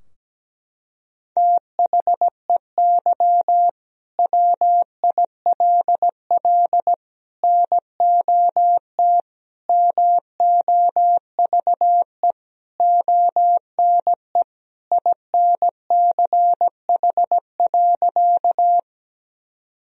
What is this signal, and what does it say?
Morse code, heard as a keyed tone, about 17 wpm, 700 Hz. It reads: THEY WILL NOT MOVE ONE INCH.
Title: Sentences from Top 1000 Words - Review Part 3 - 17wpm